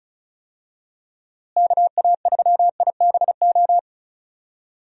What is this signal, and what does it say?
Morse code, heard as a keyed tone, about 36 wpm, 700 Hz. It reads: KA3IBO
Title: USA Call Signs - 6 Characters - Repeated 2x - 35wpm